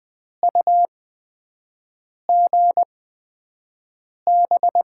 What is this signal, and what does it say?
Morse code, heard as a keyed tone, about 20 wpm, spaced as for 5 wpm, 700 Hz. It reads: UGB
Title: USA Call Signs - 6 Characters - 8x Character Spacing - 20wpm